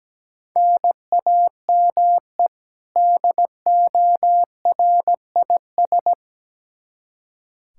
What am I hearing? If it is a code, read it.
Morse code, heard as a keyed tone, about 17 wpm, 700 Hz. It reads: NAME DORIS
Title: Practice QSO - Exchange 1 - Beginning - 17wpm